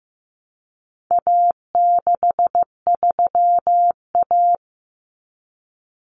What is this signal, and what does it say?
Morse code, heard as a keyed tone, about 15 wpm, 700 Hz. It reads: A63A
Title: Common Cut Numbers - 4-Digit 15wpm